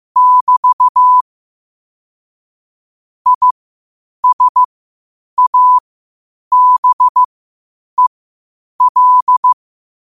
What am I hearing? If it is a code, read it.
Morse code, heard as a keyed tone, about 15 wpm, spaced as for 8 wpm, 1000 Hz. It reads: <BT> ISABEL
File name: qotdfiles / 8wpm